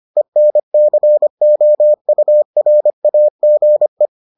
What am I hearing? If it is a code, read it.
Morse code, heard as a keyed tone, about 25 wpm, 600 Hz. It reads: ENCOURAGE